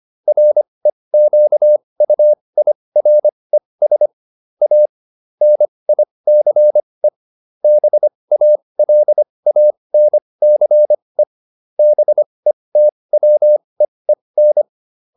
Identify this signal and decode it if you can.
Morse code, heard as a keyed tone, about 25 wpm, spaced as for 20 wpm, 600 Hz. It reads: REQUIRES A NICE BALANCE BETWEEN